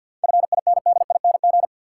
Morse code, heard as a keyed tone, about 53 wpm, 700 Hz. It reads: FINDING
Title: Sentences from Top 1100 Words - New Word Drill Alt - 50wpm